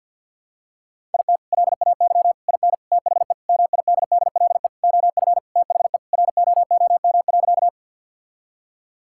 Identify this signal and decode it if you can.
Morse code, heard as a keyed tone, about 52 wpm, 700 Hz. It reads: IT LAY IN THE MIDDLE OF THE ROOM.
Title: Sentences from Top 1000 Words - Review Part 6 - 50wpm